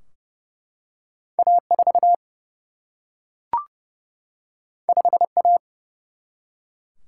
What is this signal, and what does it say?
Morse code, heard as a keyed tone, about 30 wpm, 700 Hz. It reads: A4 5A
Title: Single-Letter Single-Number 30wpm